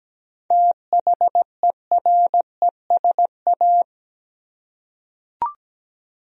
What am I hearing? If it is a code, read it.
Morse code, heard as a keyed tone, about 17 wpm, 700 Hz. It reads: THERESA
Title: USA Common Names - 17wpm